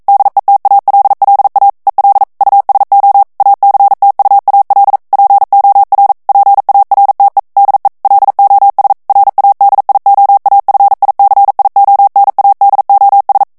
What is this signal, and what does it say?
Morse code, heard as a keyed tone, about 42 wpm, 800 Hz. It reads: DETALLA EL USO ACTUAL POR PARTE DE LOS RADIOAFICIONADOS
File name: morse mp3 / ESPAGNOL TEXTE EN CLAIR 1 VITESSE 40 MOTS MINUTES